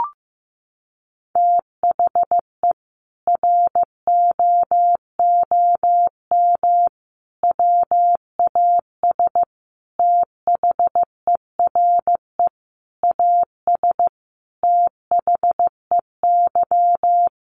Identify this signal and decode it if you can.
Morse code, heard as a keyed tone, about 15 wpm, 700 Hz. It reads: THE ROOM WAS THERE AS THEY
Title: A Sound Of Thunder 15wpm